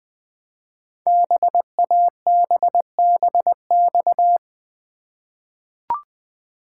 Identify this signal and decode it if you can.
Morse code, heard as a keyed tone, about 20 wpm, 700 Hz. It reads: BABBX
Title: Code Groups - 5 Letters - 20wpm